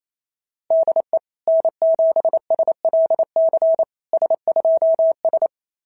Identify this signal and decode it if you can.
Morse code, heard as a keyed tone, about 25 wpm, 650 Hz. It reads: DE N7SLC S2S
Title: SOTA QSO Practice 28wpm